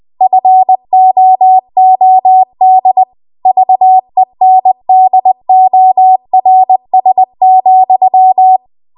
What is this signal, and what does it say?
Morse code, heard as a keyed tone, about 20 wpm, 750 Hz. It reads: FOOD VENDORS,